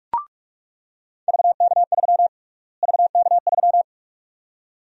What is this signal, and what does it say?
Morse code, heard as a keyed tone, about 47 wpm, 700 Hz. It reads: VK3 VK3
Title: USA Call Sign Prefixes - Repeated 2x - 45wpm